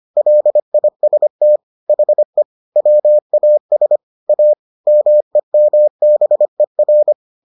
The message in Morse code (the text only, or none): LIST HE WAS A MEMBER